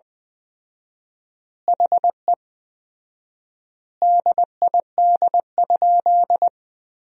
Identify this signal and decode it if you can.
Morse code, heard as a keyed tone, about 20 wpm, 700 Hz. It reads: HE DID?
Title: 4x Word-Spacing - Sentences from Top 100 Words - 20wpm